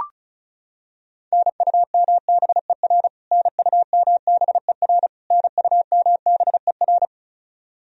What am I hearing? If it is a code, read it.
Morse code, heard as a keyed tone, about 34 wpm, 700 Hz. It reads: NUMBER NUMBER NUMBER
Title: Top 100 Words Repeated 35wpm